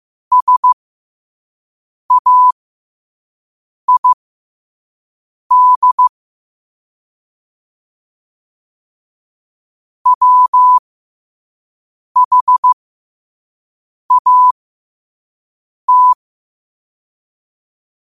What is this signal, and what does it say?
Morse code, heard as a keyed tone, about 15 wpm, spaced as for 5 wpm, 1000 Hz. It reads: SAID WHAT
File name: qotdfiles / 5wpm